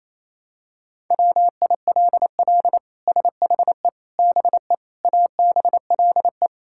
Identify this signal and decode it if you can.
Morse code, heard as a keyed tone, about 28 wpm, 700 Hz. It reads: WILL SHE BE ABLE
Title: Sentences from Top 1000 Words - Review Part 6 - 28wpm